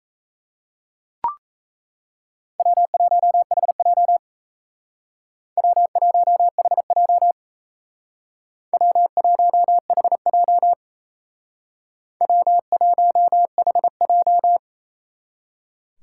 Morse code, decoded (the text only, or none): W1HJ W1HJ W1HJ W1HJ